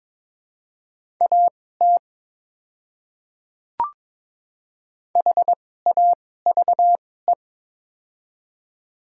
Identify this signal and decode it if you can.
Morse code, heard as a keyed tone, about 22 wpm, spaced as for 16 wpm, 700 Hz. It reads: AT HAVE
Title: 2x Character Spacing - Top 100 Words - 22wpm